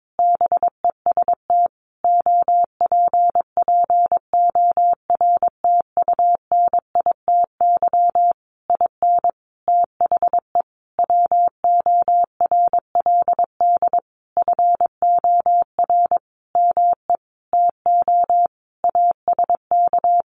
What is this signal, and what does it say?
Morse code, heard as a keyed tone, about 22 wpm, 700 Hz. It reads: BEST OPPORTUNITY IN THE WORLD FOR ME TO ASK
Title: Sentences from Top 1500 Words 22wpm